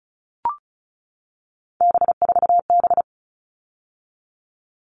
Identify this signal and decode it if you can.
Morse code, heard as a keyed tone, about 34 wpm, 700 Hz. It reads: B4B
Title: All Cut Numbers - 3-Digit 35wpm